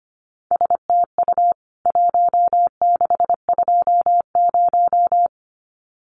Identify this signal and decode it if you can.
Morse code, heard as a keyed tone, about 25 wpm, 700 Hz. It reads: STU 1620
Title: CWT Exchange 25wpm